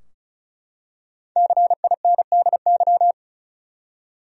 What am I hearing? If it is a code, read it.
Morse code, heard as a keyed tone, about 34 wpm, 700 Hz. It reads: CINDY
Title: USA Common Names - Max 5 Letters - 35wpm